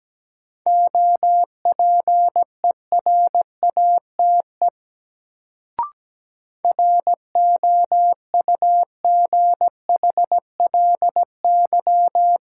Morse code, heard as a keyed tone, about 17 wpm, 700 Hz. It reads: OPERATE ROUGHLY
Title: Sentences from Top 1500 Words - New Word Drill Alt - 17wpm